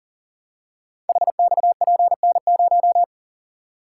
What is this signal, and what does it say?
Morse code, heard as a keyed tone, about 39 wpm, 700 Hz. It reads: HXPN0
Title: Code Groups - 5 Letter:Numbers - 40wpm